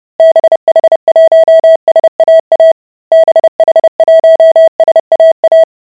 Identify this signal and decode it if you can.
Morse code, heard as a keyed tone, about 30 wpm, 650 Hz. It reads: BH1SAA BH1SAA